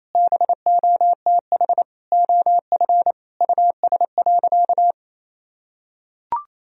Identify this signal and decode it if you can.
Morse code, heard as a keyed tone, about 28 wpm, 700 Hz. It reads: BOTH OF US.
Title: Sentences from Top 1000 Words - Review Part 1 - 28wpm